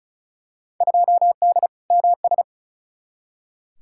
Morse code, heard as a keyed tone, about 36 wpm, 700 Hz. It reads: 2D MS
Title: Field Day Exchanges Encore - 35wpm